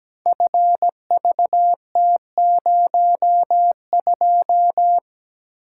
Morse code, heard as a keyed tone, about 17 wpm, 700 Hz. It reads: FVT02
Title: Code Groups - 5 Letter:Numbers - 17wpm